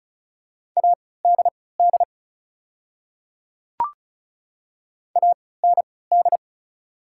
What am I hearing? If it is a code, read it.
Morse code, heard as a keyed tone, about 36 wpm, spaced as for 20 wpm, 690 Hz. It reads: ADD AND
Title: 3x Character Spacing - Top 200 Words - 35wpm